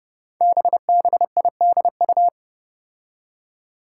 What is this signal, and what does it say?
Morse code, heard as a keyed tone, about 29 wpm, 700 Hz. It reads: BBIDU
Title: Five Challenging Letter-Number Combinations 30wpm